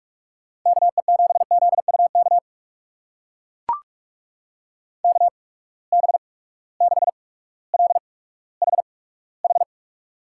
Morse code, heard as a keyed tone, about 44 wpm, spaced as for 19 wpm, 700 Hz. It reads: KE7ZUK KB6LHH
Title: USA Call Signs - 6 Characters - 8x Character Spacing - 45wpm